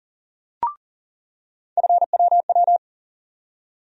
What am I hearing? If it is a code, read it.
Morse code, heard as a keyed tone, about 40 wpm, 700 Hz. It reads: FWW
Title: USA Call Sign Postfixes - 3 Letters 40wpm